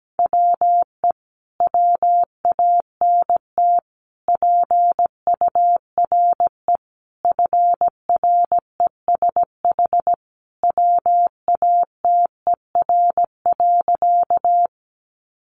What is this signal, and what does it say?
Morse code, heard as a keyed tone, about 17 wpm, 700 Hz. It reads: WE WANT PURE FRESH WATER.
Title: Sentences from Top 1500 Words 17wpm